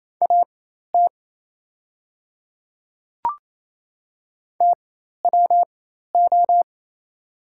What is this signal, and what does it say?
Morse code, heard as a keyed tone, about 28 wpm, spaced as for 13 wpm, 700 Hz. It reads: AT TWO
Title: 4x Character Spacing - Top 100 Words - 28wpm